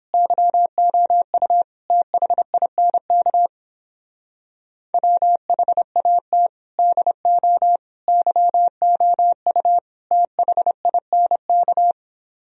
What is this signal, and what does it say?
Morse code, heard as a keyed tone, about 29 wpm, 700 Hz. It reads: YOU THINK WHAT DO YOU THINK